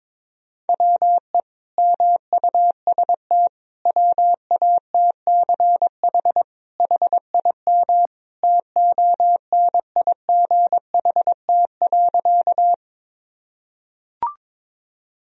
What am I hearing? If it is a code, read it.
Morse code, heard as a keyed tone, about 22 wpm, 700 Hz. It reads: WE MUST WATCH HIM TONIGHT.
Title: Sentences from Top 1000 Words - Review Part 3 - 22wpm